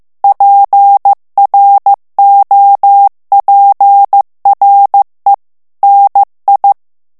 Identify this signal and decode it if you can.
Morse code, heard as a keyed tone, about 15 wpm, 800 Hz. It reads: PROPRE NI
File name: morse mp3 / FRANCAIS TEXTE CLAIR 2 VITESSE 14 MOTS MINUTES